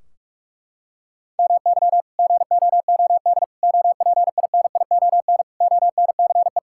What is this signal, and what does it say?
Morse code, heard as a keyed tone, about 46 wpm, 700 Hz. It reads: MY GOOD OPINION ONCE
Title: Sentences from Top 1600 Words 45wpm